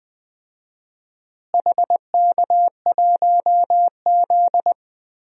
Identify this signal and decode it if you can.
Morse code, heard as a keyed tone, about 20 wpm, 700 Hz. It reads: HK1Z
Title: DXCC-100 Call Signs - 4 Characters - Repeated 3x - 20wpm